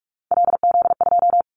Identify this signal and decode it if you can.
Morse code, heard as a keyed tone, about 46 wpm, 700 Hz. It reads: LZ2
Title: Callsign Prefixes for Top 100 DXCC - 45wpm